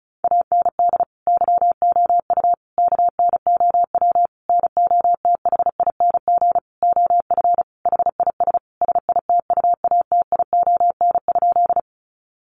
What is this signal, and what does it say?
Morse code, heard as a keyed tone, about 34 wpm, 700 Hz. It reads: AND YOU KNOW NOTHING OF HIS SITUATION?
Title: Sentences from Top 1200 Words 35wpm